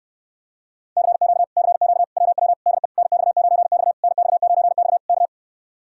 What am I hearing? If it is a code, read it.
Morse code, heard as a keyed tone, about 63 wpm, 700 Hz. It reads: CQ CQ WW DE NX9X NX9X K